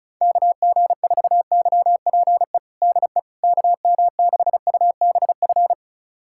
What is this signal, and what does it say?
Morse code, heard as a keyed tone, about 34 wpm, 700 Hz. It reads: KG4YPE DE KM6UBF
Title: USA Callsign DE Callsign 35wpm